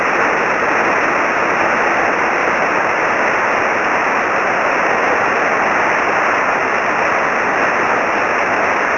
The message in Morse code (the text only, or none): none